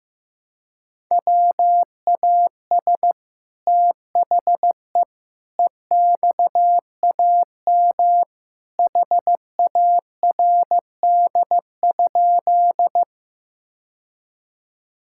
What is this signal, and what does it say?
Morse code, heard as a keyed tone, about 15 wpm, 700 Hz. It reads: WAS THE EXAM HARD?
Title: Sentences from Top 1000 Words - Review Part 5 - 15wpm